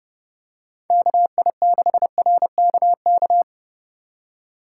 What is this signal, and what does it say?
Morse code, heard as a keyed tone, about 30 wpm, 700 Hz. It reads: KI6RKK